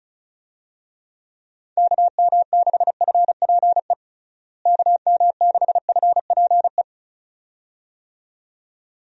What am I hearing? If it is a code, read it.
Morse code, heard as a keyed tone, about 36 wpm, 700 Hz. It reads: KM6FPE KM6FPE
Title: USA Call Signs - 6 Characters - Repeated 2x - 35wpm